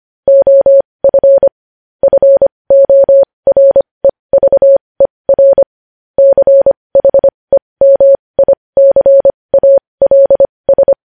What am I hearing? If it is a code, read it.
Morse code, heard as a keyed tone, about 25 wpm, spaced as for 20 wpm, 550 Hz. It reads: OF FOREVER CHEMICALS
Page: News Headlines in Morse Code at 20 WPM